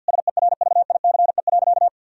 Morse code, heard as a keyed tone, about 49 wpm, 700 Hz. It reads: SERVICE.
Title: Sentences from Top 1000 Words - Review Part 4 - 50wpm